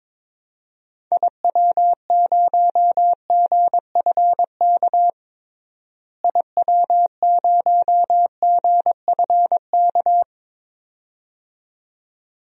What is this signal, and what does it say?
Morse code, heard as a keyed tone, about 22 wpm, 700 Hz. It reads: IW0GFK IW0GFK
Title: DXCC-100 Call Signs - 6 Characters - Repeated 2x - 22wpm